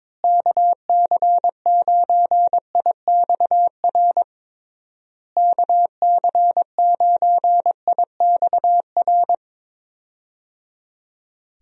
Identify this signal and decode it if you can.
Morse code, heard as a keyed tone, about 22 wpm, 700 Hz. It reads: KC9IXR KC9IXR
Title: USA Call Signs - 6 Characters - Repeated 3x - 22wpm